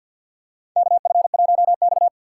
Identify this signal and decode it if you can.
Morse code, heard as a keyed tone, about 50 wpm, 700 Hz. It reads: KF1X
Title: USA Call Signs - 2x Character Spacing - 50wpm